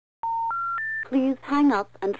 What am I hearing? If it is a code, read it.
Special information tone, with speech.